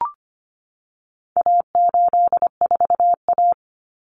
Morse code, heard as a keyed tone, about 25 wpm, 700 Hz. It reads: A84A
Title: Common Cut Numbers - 4-Digit 25wpm